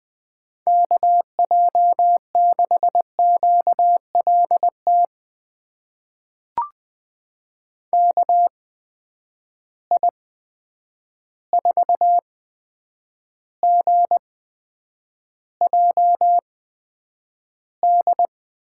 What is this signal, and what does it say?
Morse code, heard as a keyed tone, about 20 wpm, spaced as for 9 wpm, 700 Hz. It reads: KJ6QLT KI4GJD
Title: USA Call Signs - 6 Characters - 8x Character Spacing - 20wpm